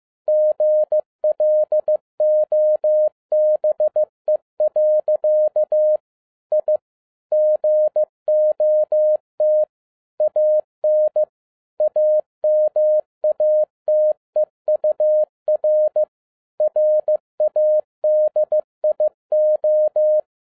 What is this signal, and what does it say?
Morse code, heard as a keyed tone, about 15 wpm, 610 Hz. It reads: GLOBE. I GOT AN AMATEUR RADIO